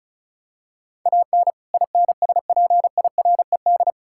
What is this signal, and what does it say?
Morse code, heard as a keyed tone, about 36 wpm, 700 Hz. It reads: AN INSPIRED
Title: Pangram Challenge v2 - 35wpm